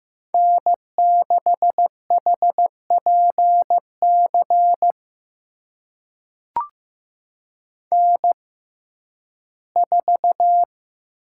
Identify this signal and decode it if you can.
Morse code, heard as a keyed tone, about 15 wpm, spaced as for 11 wpm, 700 Hz. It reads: N6HPC N4
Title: USA Call Signs - 5 Characters - 6x Character Spacing - 15wpm